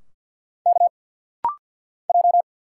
Morse code, heard as a keyed tone, about 50 wpm, 700 Hz. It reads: K J
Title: Single Letters - Rapid-Fire 50wpm